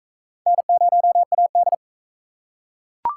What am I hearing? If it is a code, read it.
Morse code, heard as a keyed tone, about 38 wpm, 700 Hz. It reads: N0AD